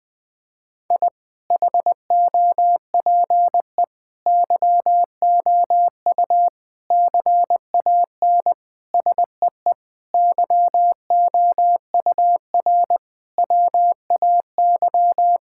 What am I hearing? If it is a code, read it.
Morse code, heard as a keyed tone, about 20 wpm, 700 Hz. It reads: I HOPE YOU CAN SEE YOUR WAY
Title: Sentences with 8-Word Phrases - 20wpm